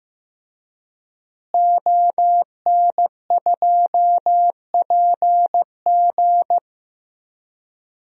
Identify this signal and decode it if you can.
Morse code, heard as a keyed tone, about 15 wpm, 700 Hz. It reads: ON2PG
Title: DXCC-100 Call Signs - 5 Characters - Repeated 3x - 15wpm